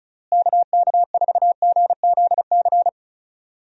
Morse code, heard as a keyed tone, about 36 wpm, 700 Hz. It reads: KK4GZC
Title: USA Call Signs - 6 Characters - Repeated 2x - 35wpm